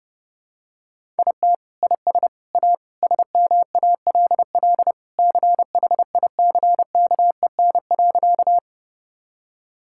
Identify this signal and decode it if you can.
Morse code, heard as a keyed tone, about 32 wpm, 700 Hz. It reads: IT IS A SMALL CHICKEN.
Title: Sentences from Top 1000 Words - Review Part 5 - 30wpm